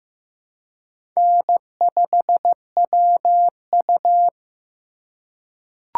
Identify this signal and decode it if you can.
Morse code, heard as a keyed tone, about 15 wpm, 700 Hz. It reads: N5WU
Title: USA Call Signs - 4 Characters - 3x Character Spacing - 15wpm